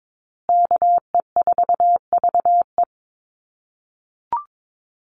Morse code, heard as a keyed tone, about 22 wpm, 700 Hz. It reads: KE4VE
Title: USA Call Signs Repeated 3x 22wpm